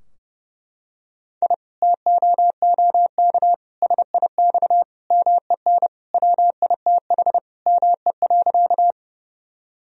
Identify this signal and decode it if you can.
Morse code, heard as a keyed tone, about 30 wpm, 700 Hz. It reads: I TOOK SIX MEN WITH ME.
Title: Sentences from Top 1000 Words - Review Part 2 - 30wpm